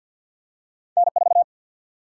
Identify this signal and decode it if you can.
Morse code, heard as a keyed tone, about 50 wpm, 700 Hz. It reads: N4
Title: Single-Letter Single-Number 50wpm